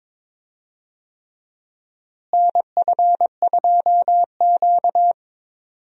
Morse code, heard as a keyed tone, about 22 wpm, 700 Hz. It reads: NF2Q
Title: USA Call Signs - 4 Characters - Repeated 3x - 22wpm